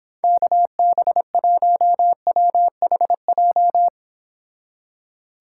Morse code, heard as a keyed tone, about 26 wpm, 700 Hz. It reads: KB1WHJ